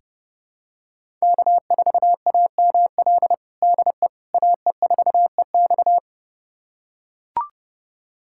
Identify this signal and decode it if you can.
Morse code, heard as a keyed tone, about 30 wpm, 700 Hz. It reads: K4AML DE AE4EX
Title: USA Callsign DE Callsign - 5 Characters - 30wpm